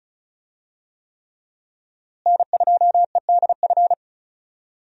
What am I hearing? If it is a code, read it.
Morse code, heard as a keyed tone, about 34 wpm, 700 Hz. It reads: N2EDF
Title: USA Call Signs - 5 Characters - Repeated 3x - 35wpm